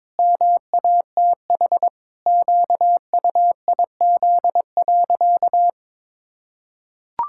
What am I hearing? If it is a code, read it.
Morse code, heard as a keyed tone, about 22 wpm, 700 Hz. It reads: MATH QUIZ.
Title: Pangram Challenge v2 - 22wpm